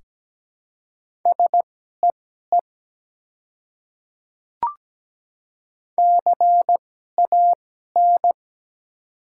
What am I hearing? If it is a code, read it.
Morse code, heard as a keyed tone, about 17 wpm, spaced as for 12 wpm, 700 Hz. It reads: SEE CAN
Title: 2x Character Spacing - Top 100 Words - 17wpm